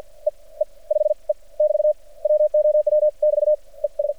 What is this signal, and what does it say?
Morse code, heard as a keyed tone, about 56 wpm, 600 Hz. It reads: E E 5 E <BT> WOW <BT> E S
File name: archived / CRSnet